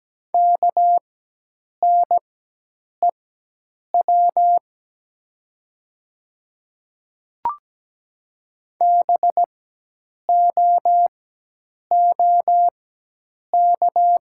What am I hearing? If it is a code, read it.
Morse code, heard as a keyed tone, about 17 wpm, spaced as for 8 wpm, 700 Hz. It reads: KNEW BOOK
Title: 4x Character Spacing - Top 500 Words - 17wpm